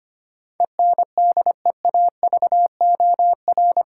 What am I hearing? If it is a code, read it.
Morse code, heard as a keyed tone, about 25 wpm, 700 Hz. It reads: ENDEAVOR
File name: 8 Letter Words - First 3 Letters Given - 25wpm